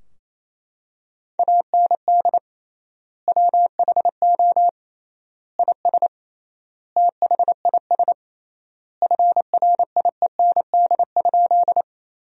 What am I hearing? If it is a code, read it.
Morse code, heard as a keyed tone, about 27 wpm, 700 Hz. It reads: AND WHO IS THIS FRIEND?
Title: 3x Word-Spacing - Sentences from Top 300 Words - 28wpm